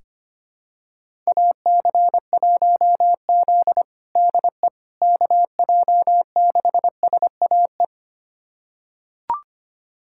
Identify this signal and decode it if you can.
Morse code, heard as a keyed tone, about 25 wpm, 700 Hz. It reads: AC1Z DE KJ6SAE
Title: USA Callsign DE Callsign 25wpm